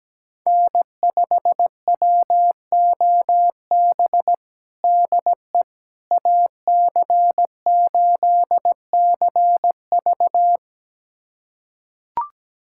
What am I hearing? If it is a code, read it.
Morse code, heard as a keyed tone, about 17 wpm, 700 Hz. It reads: N5WOB DE AC8CV
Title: USA Callsign DE Callsign - 5 Characters - 17wpm